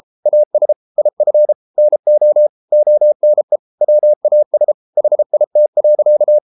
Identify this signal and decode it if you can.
Morse code, heard as a keyed tone, about 33 wpm, 600 Hz. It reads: AS IF NO ONE WAS HIT.